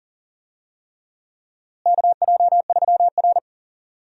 Morse code, heard as a keyed tone, about 40 wpm, 700 Hz. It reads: KJ3R
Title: USA Call Signs - 4 Characters - Repeated 3x - 40wpm